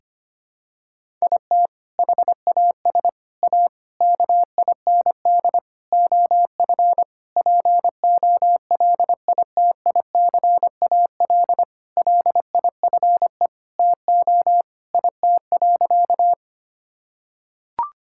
Morse code, heard as a keyed tone, about 25 wpm, 700 Hz. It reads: IT HAS A KIND OF POLITICAL LIFE TO IT.
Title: Sentences from Top 1000 Words - Review Part 5 - 25wpm